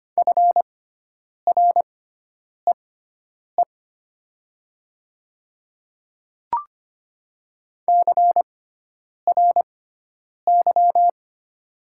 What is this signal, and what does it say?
Morse code, heard as a keyed tone, about 25 wpm, spaced as for 9 wpm, 700 Hz. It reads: FREE CRY
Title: 6x Character Spacing - Top 500 Words - 25wpm